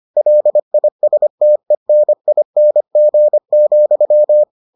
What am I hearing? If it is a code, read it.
Morse code, heard as a keyed tone, about 25 wpm, 600 Hz. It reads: LISTENING,